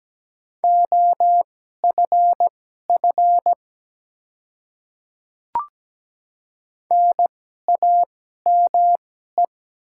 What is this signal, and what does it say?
Morse code, heard as a keyed tone, about 17 wpm, spaced as for 12 wpm, 700 Hz. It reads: OFF NAME
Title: 2x Character Spacing - Top 200 Words - 17wpm